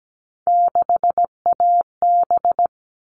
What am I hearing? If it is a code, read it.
Morse code, heard as a keyed tone, about 17 wpm, 700 Hz. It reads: 6AB
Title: All Cut Numbers - 3-Digit 17wpm